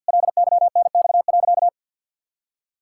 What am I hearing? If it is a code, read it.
Morse code, heard as a keyed tone, about 50 wpm, 700 Hz. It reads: LYNX.
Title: Pangram Challenge v2 - 50wpm